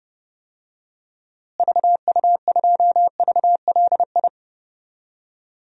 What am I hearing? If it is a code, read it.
Morse code, heard as a keyed tone, about 31 wpm, 700 Hz. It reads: VU2VLI
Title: DXCC-100 Call Signs - 6 Characters - Repeated 2x - 30wpm